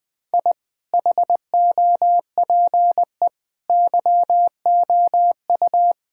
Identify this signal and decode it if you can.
Morse code, heard as a keyed tone, about 19 wpm, 700 Hz. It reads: I HOPE YOU
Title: Sentences with 8-Word Phrases - Spoken Beginning - 20wpm